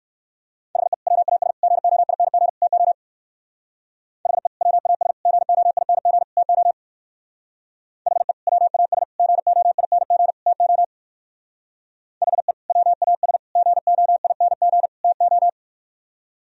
Morse code, heard as a keyed tone, about 55 wpm, 700 Hz. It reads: HE WAS GOING TO HE WAS GOING TO HE WAS GOING TO HE WAS GOING TO
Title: Speed-Racing - Sets of 4 Words 45wpm